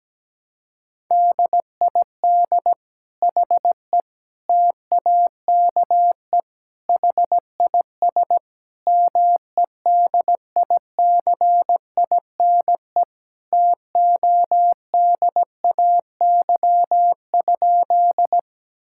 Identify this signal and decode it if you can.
Morse code, heard as a keyed tone, about 17 wpm, 700 Hz. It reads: DID HE TAKE HIS MEDICINE TODAY?
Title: Sentences from Top 1000 Words - Review Part 6 - 17wpm